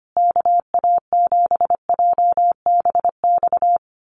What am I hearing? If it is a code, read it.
Morse code, heard as a keyed tone, about 25 wpm, 700 Hz. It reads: KA7JBX
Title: USA Call Signs Repeated 3x 25wpm